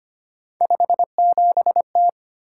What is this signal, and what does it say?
Morse code, heard as a keyed tone, about 25 wpm, 700 Hz. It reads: 57T
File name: Three Letter-Number Combinations - 25wpm